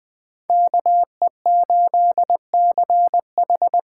Morse code, heard as a keyed tone, about 20 wpm, 700 Hz. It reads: KE8CH
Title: USA Call Signs - 8x Character Spacing - 20wpm